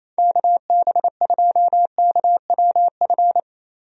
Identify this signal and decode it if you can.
Morse code, heard as a keyed tone, about 27 wpm, 700 Hz. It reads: KB2KWF
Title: USA Call Signs - 6 Characters - Repeated 2x - 28wpm